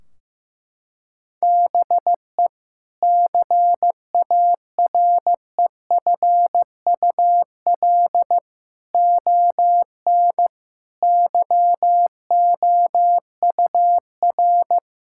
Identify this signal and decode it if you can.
Morse code, heard as a keyed tone, about 15 wpm, 700 Hz. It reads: BE CAREFUL ON YOUR
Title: Sentences from Top 1600 Words 15wpm